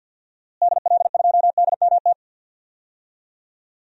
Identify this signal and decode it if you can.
Morse code, heard as a keyed tone, about 47 wpm, 700 Hz. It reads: DL2DMT
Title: DXCC-100 Call Signs - 6 Characters - Repeated 2x - 50wpm